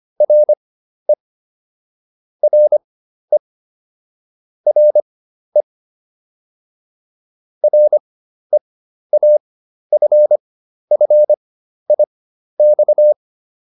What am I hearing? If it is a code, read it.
Morse code, heard as a keyed tone, about 25 wpm, spaced as for 12 wpm, 600 Hz. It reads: RE RE RE REAFFIX